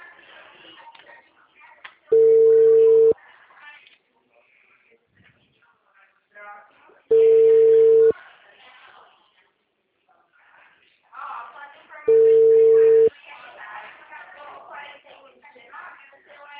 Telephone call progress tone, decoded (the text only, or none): ringback tone